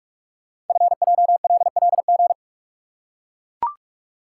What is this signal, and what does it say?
Morse code, heard as a keyed tone, about 44 wpm, 700 Hz. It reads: FJLLG
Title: Code Groups - 5 Letters - 45wpm